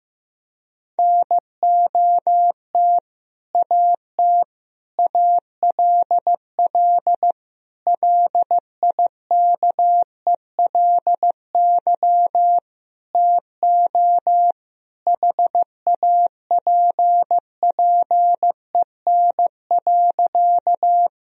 Morse code, heard as a keyed tone, about 16 wpm, 700 Hz. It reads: NOT AT ALL LIKELY TO HAPPEN.
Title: Sentences from Top 1000 Words - Review Part 4 - 15wpm